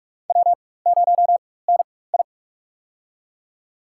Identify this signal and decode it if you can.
Morse code, heard as a keyed tone, about 43 wpm, spaced as for 21 wpm, 700 Hz. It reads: W0NI
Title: USA Call Signs - 4 Characters - 4x Character Spacing - 45wpm